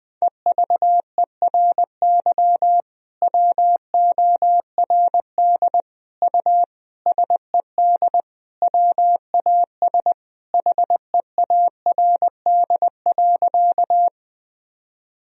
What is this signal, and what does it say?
Morse code, heard as a keyed tone, about 21 wpm, 700 Hz. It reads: EVERY WORD U SED WAS HEARD.
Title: Sentences from Top 400 Words and Abbreviations 20wpm